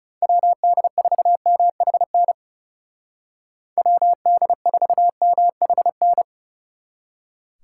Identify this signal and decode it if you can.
Morse code, heard as a keyed tone, about 37 wpm, 700 Hz. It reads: WD4MHN WD4MHN
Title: Speed Racing - USA 2x3 Call Signs 30wpm